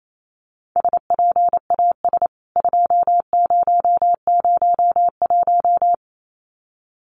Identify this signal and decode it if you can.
Morse code, heard as a keyed tone, about 28 wpm, 700 Hz. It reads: SPAS 2001
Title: CWT Exchange 28wpm